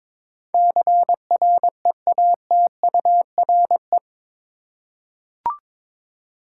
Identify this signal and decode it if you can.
Morse code, heard as a keyed tone, about 22 wpm, 700 Hz. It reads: CREATURE
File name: practice / 22wpm / Eight Letter Words - 22wpm